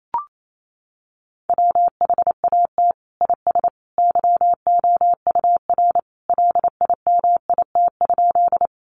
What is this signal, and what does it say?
Morse code, heard as a keyed tone, about 28 wpm, 700 Hz. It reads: WHAT IS YOUR LIMIT?
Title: Sentences from Top 1500 Words 28wpm